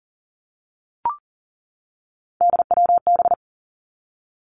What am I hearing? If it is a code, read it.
Morse code, heard as a keyed tone, about 40 wpm, 700 Hz. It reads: DWB
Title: Three Challenging Letter-Number Combinations 40wpm